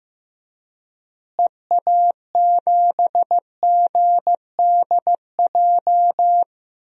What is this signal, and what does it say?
Morse code, heard as a keyed tone, about 15 wpm, 700 Hz. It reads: EA7GDJ
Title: DXCC-100 Call Signs - 6 Characters - Repeated 3x - 15wpm